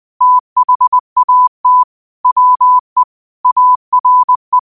Morse code, heard as a keyed tone, about 20 wpm, 1000 Hz. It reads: THAT WE ARE
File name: qotdfiles / 20wpm